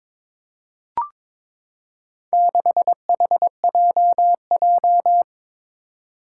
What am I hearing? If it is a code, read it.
Morse code, heard as a keyed tone, about 23 wpm, 700 Hz. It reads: 6HJJ
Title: Four Challenging Letter-Number Combinations 22wpm